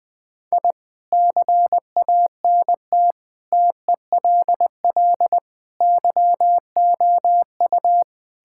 Morse code, heard as a keyed tone, about 21 wpm, 700 Hz. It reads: I CANT TELL YOU
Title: Sets of 8 Words - 20wpm